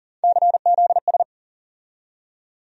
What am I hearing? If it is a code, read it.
Morse code, heard as a keyed tone, about 40 wpm, 700 Hz. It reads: CZS
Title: USA Call Sign Postfixes - 3 Letters 40wpm